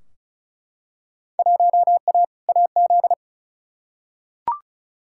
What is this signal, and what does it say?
Morse code, heard as a keyed tone, about 36 wpm, 700 Hz. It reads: 1A AZ
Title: Field Day Exchanges - 35wpm